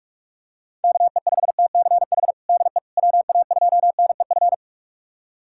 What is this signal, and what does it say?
Morse code, heard as a keyed tone, about 46 wpm, 690 Hz. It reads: KE5TCH DE WA2NEF
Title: USA Callsign DE Callsign 45wpm